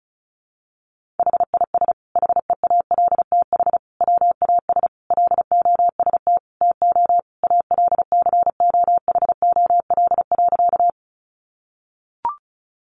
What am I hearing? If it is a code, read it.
Morse code, heard as a keyed tone, about 34 wpm, 700 Hz. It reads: HIS HEALTH WAS LOST TO ALCOHOL.
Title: Sentences from Top 1400 Words 35wpm